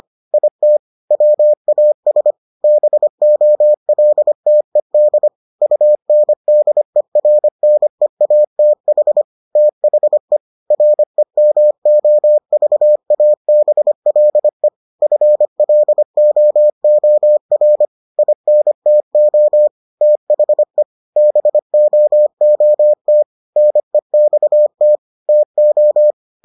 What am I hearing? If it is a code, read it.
Morse code, heard as a keyed tone, about 25 wpm, 600 Hz. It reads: IT WAS BOLTED UNDERNEATH THE REMOVABLE FLOOR INTO THE BOOT NEXT TO